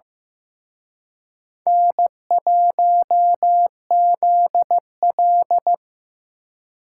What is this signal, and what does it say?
Morse code, heard as a keyed tone, about 18 wpm, 700 Hz. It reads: N1ZL
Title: Speed Racing - USA 1x2 Call Signs 15wpm